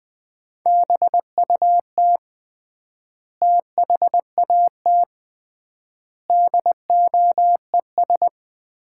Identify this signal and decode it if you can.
Morse code, heard as a keyed tone, about 20 wpm, 700 Hz. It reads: BUT THAT DOES
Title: 3x Word-Spacing - Sets of 5 Words - 20wpm